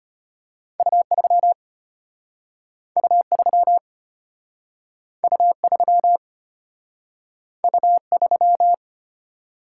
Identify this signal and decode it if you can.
Morse code, heard as a keyed tone, about 33 wpm, 700 Hz. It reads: U3 U3 U3 U3